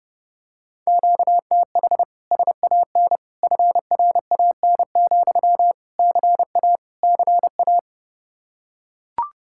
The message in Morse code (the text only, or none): QTH SAN FRAN, CA CA